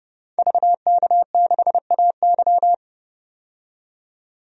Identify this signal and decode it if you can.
Morse code, heard as a keyed tone, about 32 wpm, 700 Hz. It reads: VK6AY
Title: DXCC-100 Call Signs - 5 Characters - 30wpm